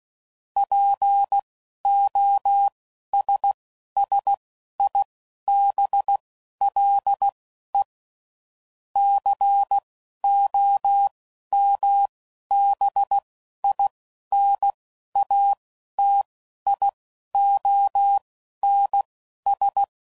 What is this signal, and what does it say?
Morse code, heard as a keyed tone, about 16 wpm, spaced as for 11 wpm, 800 Hz. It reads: POSSIBLE COMBINATIONS
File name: CW Code Groups @10WPM / Technical Subjects / AND Gate